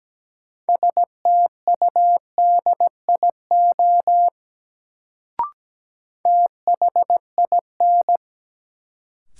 Morse code, heard as a keyed tone, about 17 wpm, 700 Hz. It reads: STUDIO THIN
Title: Sentences from Top 1200 Words - New Word Drill Alt - 17wpm